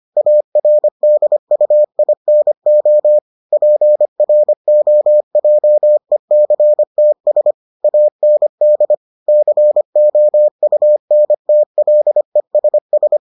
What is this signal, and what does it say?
Morse code, heard as a keyed tone, about 25 wpm, 600 Hz. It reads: ARDUINO PROJECTS AND COUNTLESS